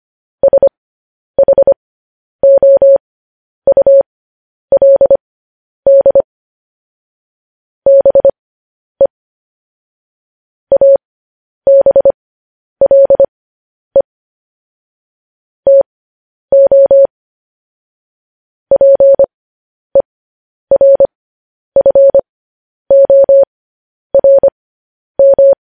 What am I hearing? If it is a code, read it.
Morse code, heard as a keyed tone, about 25 wpm, spaced as for 10 wpm, 550 Hz. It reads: SHOULD BE ABLE TO PERFORM